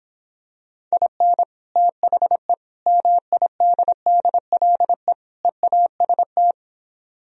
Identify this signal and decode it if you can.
Morse code, heard as a keyed tone, about 26 wpm, 700 Hz. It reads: IN THE MIDDLE EAST